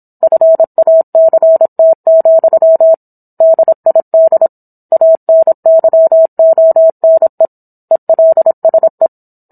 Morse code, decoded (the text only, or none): FACT, DID ANYONE ELSE